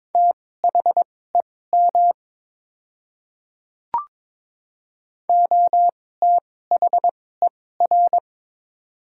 Morse code, heard as a keyed tone, about 22 wpm, spaced as for 16 wpm, 700 Hz. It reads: THEM OTHER
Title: 2x Character Spacing - Top 100 Words - 22wpm